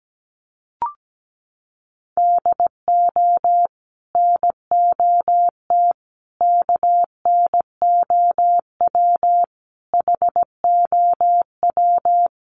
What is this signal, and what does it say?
Morse code, heard as a keyed tone, about 17 wpm, 700 Hz. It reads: DO NOT KNOW HOW
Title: Sets of 5 Words Encore2 17wpm